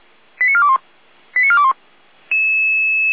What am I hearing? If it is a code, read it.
Five-tone selective call (ZVEI), heard as quick runs of five tones, tones of about 70 ms. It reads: 89421 89421